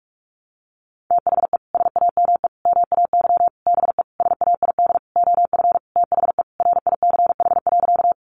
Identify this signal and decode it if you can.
Morse code, heard as a keyed tone, about 44 wpm, 700 Hz. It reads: THE SAME MAY BE SAID OF THE RICH.
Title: Sentences from Top 1100 Words 45wpm